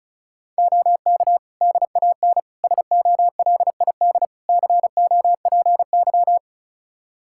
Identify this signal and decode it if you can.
Morse code, heard as a keyed tone, about 36 wpm, 700 Hz. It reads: OK DAN SOLID COPY
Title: Practice QSO - Exchange 3 - More Info - 35wpm